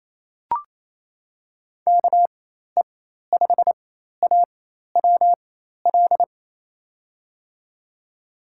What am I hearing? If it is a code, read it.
Morse code, heard as a keyed tone, about 28 wpm, spaced as for 13 wpm, 700 Hz. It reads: KE5AWL
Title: USA Call Signs - 4x Character Spacing - 28wpm